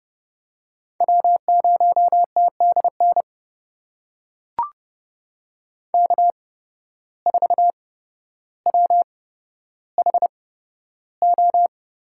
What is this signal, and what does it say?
Morse code, heard as a keyed tone, about 30 wpm, spaced as for 13 wpm, 700 Hz. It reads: W0TDN K4WHO
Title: USA Call Signs - 5 Characters - 8x Character Spacing - 30wpm